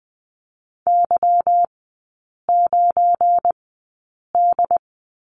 Morse code, heard as a keyed tone, about 20 wpm, spaced as for 8 wpm, 700 Hz. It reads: Y9D
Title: 205 Lesson 33e - Letter j - Three Characters 20wpm